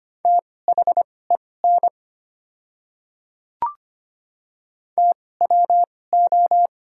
Morse code, heard as a keyed tone, about 25 wpm, spaced as for 18 wpm, 700 Hz. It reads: THEN TWO
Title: 2x Character Spacing - Top 100 Words - 25wpm